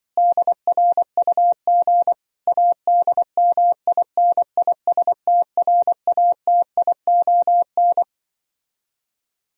Morse code, heard as a keyed tone, about 25 wpm, 700 Hz. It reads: DRUG ADMINISTRATION